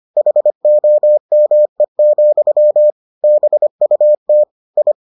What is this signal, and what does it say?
Morse code, heard as a keyed tone, about 25 wpm, 600 Hz. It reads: HOME, BUT I